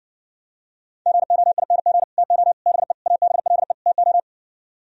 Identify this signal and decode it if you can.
Morse code, heard as a keyed tone, about 59 wpm, 700 Hz. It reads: GOING TO BE ABLE TO